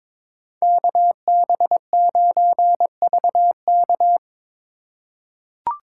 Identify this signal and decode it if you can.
Morse code, heard as a keyed tone, about 22 wpm, 700 Hz. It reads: KB9VK
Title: USA Call Signs - 4x Character Spacing - 22wpm